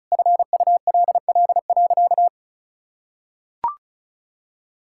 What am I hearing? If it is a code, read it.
Morse code, heard as a keyed tone, about 36 wpm, 690 Hz. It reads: FULL.
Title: Sentences from Top 1600 Words 35wpm